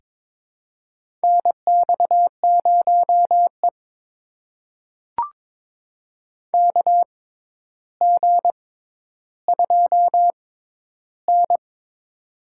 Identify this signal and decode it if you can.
Morse code, heard as a keyed tone, about 22 wpm, spaced as for 11 wpm, 700 Hz. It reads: NX0E KG2N
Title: USA Call Signs - 4 Characters - 6x Character Spacing - 22wpm